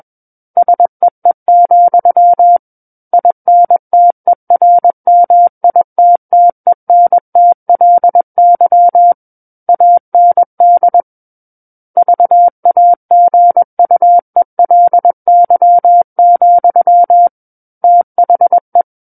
Morse code, heard as a keyed tone, about 21 wpm, 700 Hz. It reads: SEE, INTERMITTENTLY AND VAGUELY, THE
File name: Book One / 12-Chapter 12 - 21 WPM